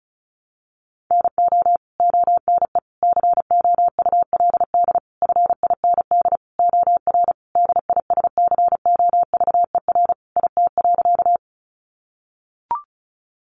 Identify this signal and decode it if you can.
Morse code, heard as a keyed tone, about 34 wpm, 700 Hz. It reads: NO ONE COULD FIND OR DISCOVER IT.
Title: Sentences from Top 1100 Words 35wpm